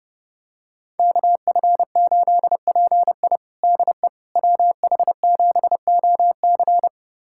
Morse code, heard as a keyed tone, about 32 wpm, 700 Hz. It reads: KF8PI DE WH7OC
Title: USA Callsign DE Callsign - 5 Characters - 30wpm